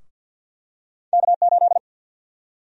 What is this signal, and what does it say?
Morse code, heard as a keyed tone, about 52 wpm, 700 Hz. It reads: K8
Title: USA Call Sign Prefixes - Repeated 2x - 50wpm